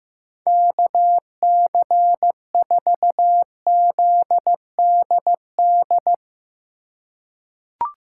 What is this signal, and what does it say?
Morse code, heard as a keyed tone, about 15 wpm, 700 Hz. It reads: KC4ZDD
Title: USA Call Signs - 6 Characters - 3x Character Spacing - 15wpm